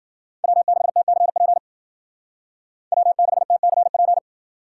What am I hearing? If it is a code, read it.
Morse code, heard as a keyed tone, about 52 wpm, 700 Hz. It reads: W6TCP W6TCP